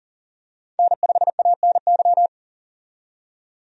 Morse code, heard as a keyed tone, about 40 wpm, 700 Hz. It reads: N5ANY
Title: USA Call Signs - 5 Characters - 40wpm